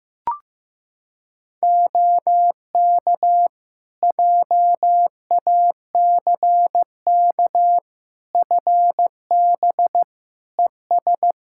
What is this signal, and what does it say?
Morse code, heard as a keyed tone, about 15 wpm, 700 Hz. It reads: OK JACK FB ES
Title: Practice QSO - Exchange 2 - Info - 15wpm